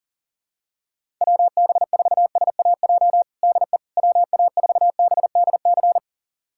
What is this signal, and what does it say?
Morse code, heard as a keyed tone, about 37 wpm, 700 Hz. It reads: WB4SAJ DE WA4BDC
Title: USA Callsign DE Callsign - 6 Characters - 40wpm